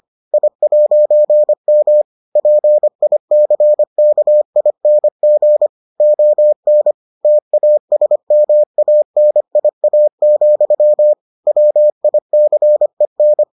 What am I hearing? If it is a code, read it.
Morse code, heard as a keyed tone, about 25 wpm, 600 Hz. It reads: I'M PICKING ON TASMANIA, WICEN